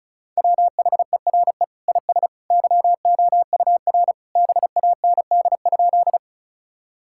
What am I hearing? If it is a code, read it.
Morse code, heard as a keyed tone, about 36 wpm, 700 Hz. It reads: WHERE IS YOUR BAND?
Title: Sentences from Top 1600 Words 35wpm